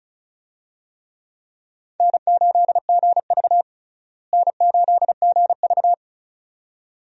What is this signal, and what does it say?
Morse code, heard as a keyed tone, about 36 wpm, 700 Hz. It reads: N8GV N8GV
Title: USA Call Signs - 4 Characters - Repeated 2x - 35wpm